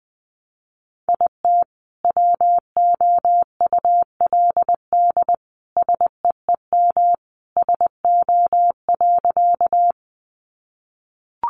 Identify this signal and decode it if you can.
Morse code, heard as a keyed tone, about 20 wpm, 700 Hz. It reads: IT WOULD SEEM SO.
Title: Sentences from Top 300 Words 20wpm